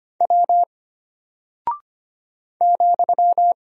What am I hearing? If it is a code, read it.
Morse code, heard as a keyed tone, about 25 wpm, 700 Hz. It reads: W ,
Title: Single Character (Letter:Number:Punctuation:Prosign) Alt 25wpm